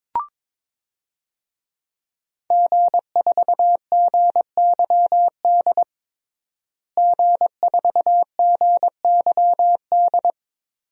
Morse code, heard as a keyed tone, about 22 wpm, 700 Hz. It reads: G4GYD G4GYD
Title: DXCC-100 Call Signs - 5 Characters - Repeated 2x - 22wpm